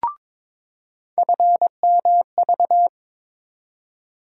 Morse code, heard as a keyed tone, about 22 wpm, 700 Hz. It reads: FMV
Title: USA Call Sign Postfixes - 3 Letters 22wpm